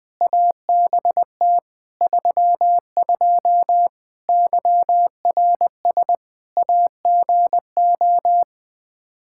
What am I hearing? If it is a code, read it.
Morse code, heard as a keyed tone, about 20 wpm, 700 Hz. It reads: ABT 32 YRS AGO
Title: Practice QSO - Exchange 3 - More Info - 20wpm